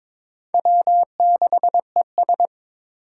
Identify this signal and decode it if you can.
Morse code, heard as a keyed tone, about 22 wpm, 700 Hz. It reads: W6ES
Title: Four Challenging Letter-Number Combinations 22wpm